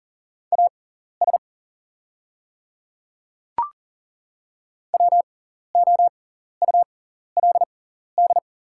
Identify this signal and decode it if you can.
Morse code, heard as a keyed tone, about 42 wpm, spaced as for 14 wpm, 700 Hz. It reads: AS WOULD
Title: 6x Character Spacing - Top 100 Words - 40wpm